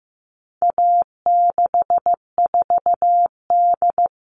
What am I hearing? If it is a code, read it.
Morse code, heard as a keyed tone, about 15 wpm, 700 Hz. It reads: A64D
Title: All Cut Numbers - 4-Digit 15wpm